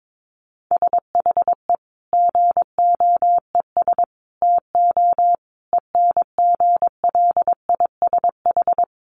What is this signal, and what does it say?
Morse code, heard as a keyed tone, about 22 wpm, 700 Hz. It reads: SHE GOES TO ENGLISH
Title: Additional Easy Phrases 22wpm